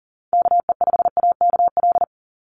Morse code, heard as a keyed tone, about 41 wpm, 700 Hz. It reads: KE5AKL
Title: NA SOTA Chaser Callsigns 40wpm